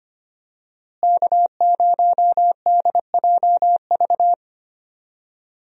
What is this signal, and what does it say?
Morse code, heard as a keyed tone, about 25 wpm, 700 Hz. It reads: K0DJV
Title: USA Call Signs - 5 Characters - Repeated 2x - 25wpm